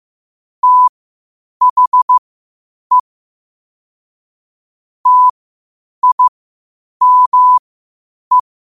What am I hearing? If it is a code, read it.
Morse code, heard as a keyed tone, about 15 wpm, spaced as for 8 wpm, 1000 Hz. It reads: THE TIME
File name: qotdfiles / 8wpm